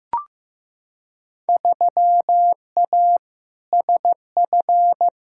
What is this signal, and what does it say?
Morse code, heard as a keyed tone, about 15 wpm, 700 Hz. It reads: 3A SF
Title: Field Day Exchanges Encore - 15wpm